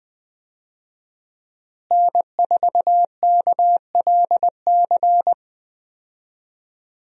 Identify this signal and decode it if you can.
Morse code, heard as a keyed tone, about 21 wpm, 700 Hz. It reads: N4KLC
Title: USA Call Signs - 5 Characters - Repeated 2x - 20wpm